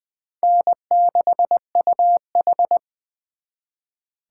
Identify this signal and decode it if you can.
Morse code, heard as a keyed tone, about 23 wpm, 700 Hz. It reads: N6UH
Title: Speed Racing - USA 1x2 Call Signs 20wpm